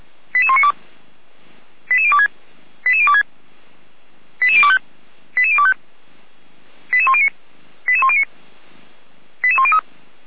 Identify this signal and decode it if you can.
Five-tone selective call (ZVEI), heard as quick runs of five tones, tones of about 70 ms. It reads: 88192 88026 88026 88025 88025 88109 88109 88103